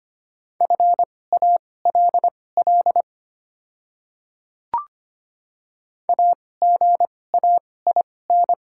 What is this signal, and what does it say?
Morse code, heard as a keyed tone, about 25 wpm, spaced as for 18 wpm, 700 Hz. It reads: FALL AGAIN
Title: 2x Character Spacing - Top 500 Words - 25wpm